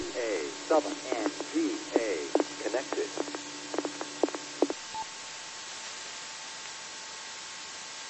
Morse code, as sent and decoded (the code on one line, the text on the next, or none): none